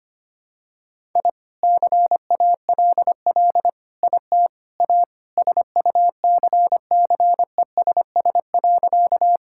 Morse code, heard as a keyed tone, about 25 wpm, 700 Hz. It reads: I CALL IT A SUCCESS.
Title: Sentences from Top 1000 Words - Review Part 6 - 25wpm